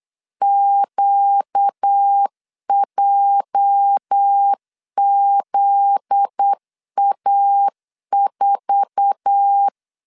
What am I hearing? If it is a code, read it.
Morse code, heard as a keyed tone, about 8 wpm, 800 Hz. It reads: QJZA4
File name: LEZIONI / 8 YJ45 / 8 YJ4 8wpm 3246